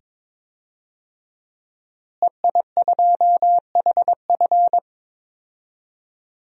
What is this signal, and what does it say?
Morse code, heard as a keyed tone, about 22 wpm, 700 Hz. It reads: EI2HF
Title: DXCC-100 Call Signs - 5 Characters - Repeated 2x - 22wpm